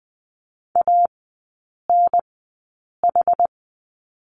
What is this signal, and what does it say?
Morse code, heard as a keyed tone, about 20 wpm, spaced as for 8 wpm, 690 Hz. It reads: ANH
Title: 060 Lesson 11e - Letter h - Three Characters 20wpm